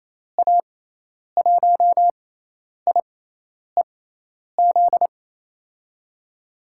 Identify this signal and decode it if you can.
Morse code, heard as a keyed tone, about 28 wpm, spaced as for 10 wpm, 700 Hz. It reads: A1IEZ
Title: USA Call Signs - 6 Characters - 6x Character Spacing - 28wpm